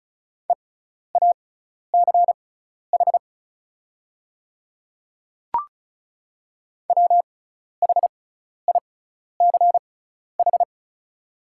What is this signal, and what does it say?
Morse code, heard as a keyed tone, about 36 wpm, spaced as for 12 wpm, 700 Hz. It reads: EACH WHICH
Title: 6x Character Spacing - Top 100 Words - 35wpm